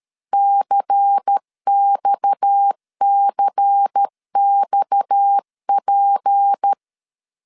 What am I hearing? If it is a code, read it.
Morse code, heard as a keyed tone, about 13 wpm, 800 Hz. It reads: CXCXP
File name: LEZIONI / 6 PKX0 / 6 PKX 12wpm 2024